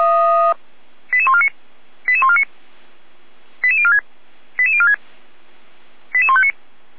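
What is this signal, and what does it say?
Five-tone selective call (ZVEI), heard as quick runs of five tones, tones of about 70 ms. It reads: 88149 88149 88046 88046 88159
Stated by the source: Five-tone ZVEI selective call received on an SDR by feuerwehr-orlamuende (fire brigade Orlamünde).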